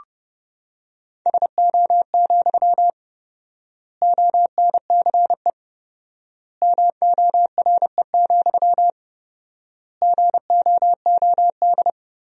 Morse code, heard as a keyed tone, about 32 wpm, 700 Hz. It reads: SO, ONCE MORE, GOOD
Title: 4x Word-Spacing - Sentences from Top 300 Words - 30wpm